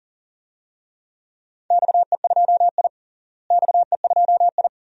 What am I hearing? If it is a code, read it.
Morse code, heard as a keyed tone, about 40 wpm, 700 Hz. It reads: XE2I XE2I
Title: DXCC-100 Call Signs - 4 Characters - Repeated 2x - 40wpm